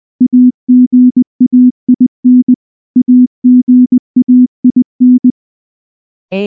Morse code, heard as a keyed tone, about 19 wpm, 250 Hz. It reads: AGAIN AGAIN
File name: morse-training-20WPM